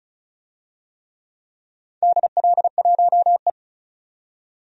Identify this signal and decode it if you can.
Morse code, heard as a keyed tone, about 36 wpm, 700 Hz. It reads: DL1E
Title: DXCC-100 Call Signs - 4 Characters - 35wpm